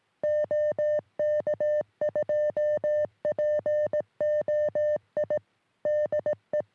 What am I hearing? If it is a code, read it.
Morse code, heard as a keyed tone, about 18 wpm, 600 Hz. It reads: OK2POI DE
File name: sm7lcb / ok2poi 3cm